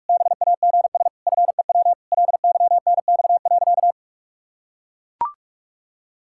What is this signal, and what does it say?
Morse code, heard as a keyed tone, about 41 wpm, 700 Hz. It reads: BAGS FEW LYNX.
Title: Pangram Challenge v2 - 45wpm